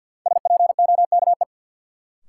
Morse code, heard as a keyed tone, about 47 wpm, 700 Hz. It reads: SPOKE
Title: Sentences from Top 1200 Words - New Word Drill - 50wpm